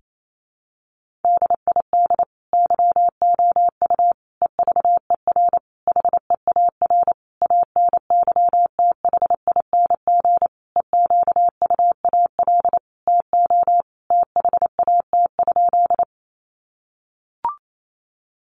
Morse code, heard as a keyed tone, about 28 wpm, 700 Hz. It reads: DID YOU EVER HEAR ANYTHING EQUAL TO THAT?
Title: Sentences from Top 1400 Words 28wpm